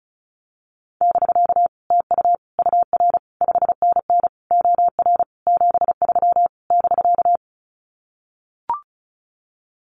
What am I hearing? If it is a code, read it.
Morse code, heard as a keyed tone, about 34 wpm, 700 Hz. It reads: <BK> TU UR 5NN OR 73 <BK>
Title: Common QSO Phrases Encore 35wpm